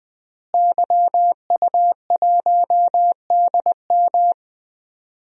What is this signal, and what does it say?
Morse code, heard as a keyed tone, about 20 wpm, 700 Hz. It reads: YU1DM
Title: DXCC-100 Call Signs - 5 Characters - Repeated 3x - 20wpm